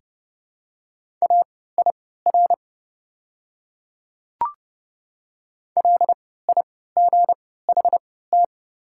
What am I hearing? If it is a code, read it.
Morse code, heard as a keyed tone, about 30 wpm, spaced as for 17 wpm, 700 Hz. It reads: AIR LIGHT
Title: 3x Character Spacing - Top 300 Words - 30wpm